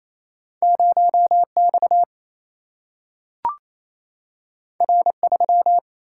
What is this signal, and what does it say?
Morse code, heard as a keyed tone, about 28 wpm, 700 Hz. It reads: 0X R3
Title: Single-Letter Single-Number 28wpm